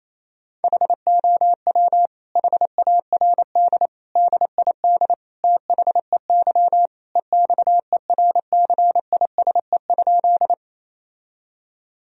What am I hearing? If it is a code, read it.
Morse code, heard as a keyed tone, about 28 wpm, 700 Hz. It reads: HOW HARD DID THEY EXERCISE?
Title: Sentences from Top 1000 Words - Review Part 4 - 28wpm